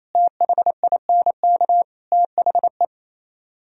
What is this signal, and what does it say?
Morse code, heard as a keyed tone, about 26 wpm, 700 Hz. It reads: THINK THE
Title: Speed-Racing - Sets of 4 Words 28wpm